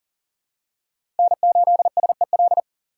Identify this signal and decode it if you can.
Morse code, heard as a keyed tone, about 40 wpm, 700 Hz. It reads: N8SEL
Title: USA Call Signs - 5 Characters - 6x Character Spacing - 40wpm